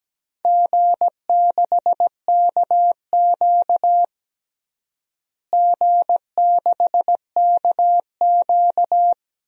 Morse code, heard as a keyed tone, about 17 wpm, 700 Hz. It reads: G6KQ G6KQ
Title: DXCC-100 Call Signs - 4 Characters - Repeated 3x - 17wpm